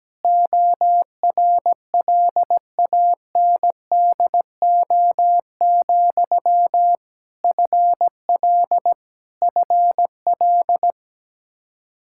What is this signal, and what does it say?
Morse code, heard as a keyed tone, about 17 wpm, 700 Hz. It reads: ORLANDO, FL FL
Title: Practice QSO - Exchange 1 - Beginning - 17wpm